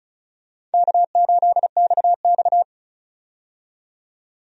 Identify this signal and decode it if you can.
Morse code, heard as a keyed tone, about 34 wpm, 700 Hz. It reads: K8XX
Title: USA Call Signs - 4 Characters - 35wpm